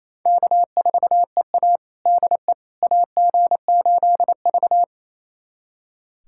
Morse code, heard as a keyed tone, about 28 wpm, 700 Hz. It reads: K4EA DE AG8V
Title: USA Callsign DE Callsign - 4 Characters - 28wpm